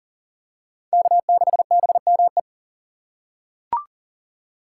Morse code, heard as a keyed tone, about 40 wpm, 700 Hz. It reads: K6BME
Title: USA Call Signs - 3x Character Spacing - 40wpm